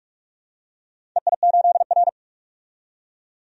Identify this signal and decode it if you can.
Morse code, heard as a keyed tone, about 46 wpm, 700 Hz. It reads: EI8R
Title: DXCC-100 Call Signs - 4 Characters - 45wpm